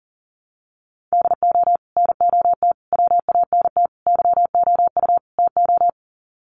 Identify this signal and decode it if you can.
Morse code, heard as a keyed tone, about 40 wpm, 700 Hz. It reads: DO NOT WANT YOU TO
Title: Sets of 5 Words 40wpm